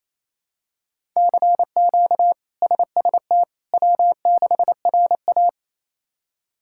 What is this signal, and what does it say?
Morse code, heard as a keyed tone, about 28 wpm, 700 Hz. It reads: CQ SST W6RA
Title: K1USN SST QSO Practice - 28wpm